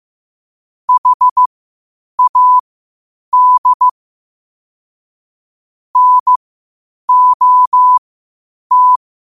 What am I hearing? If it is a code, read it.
Morse code, heard as a keyed tone, about 15 wpm, spaced as for 8 wpm, 1000 Hz. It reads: HAD NOT
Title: 8 WPM morse code quotes for Fri, 01 Mar 2024